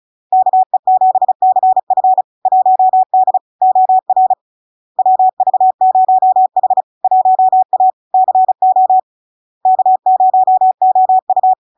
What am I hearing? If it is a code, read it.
Morse code, heard as a keyed tone, about 35 wpm, 750 Hz. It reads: KE7CF 1D OR WV0H 1A CO K0OU
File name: LARK-FD-2022-Log-in-CW-35wpm-750-Hz